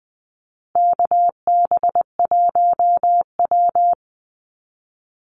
Morse code, heard as a keyed tone, about 20 wpm, 700 Hz. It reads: KB1W
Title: Call Signs Repeated 3x - 20wpm